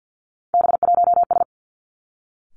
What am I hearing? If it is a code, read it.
Morse code, heard as a keyed tone, about 50 wpm, 700 Hz. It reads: B1S
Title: Three Challenging Letter-Number Combinations 50wpm